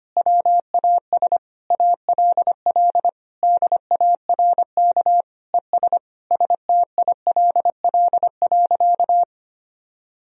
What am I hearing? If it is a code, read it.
Morse code, heard as a keyed tone, about 25 wpm, 700 Hz. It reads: WAS ALL DARK ES STILL.
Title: Sentences from Top 500 Words and Abbreviations 25wpm